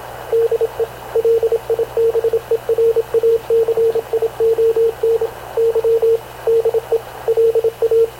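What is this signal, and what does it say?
Morse code, heard as a keyed tone, about 26 wpm, 460 Hz, with radio noise. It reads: DE LIBERACION Y DE LA
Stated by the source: Havana Radio / CLA sending Spanish press (PX) in Morse, recorded by Radio Officer Paul Dumesnil onboard CNAV QUEST/CZDO.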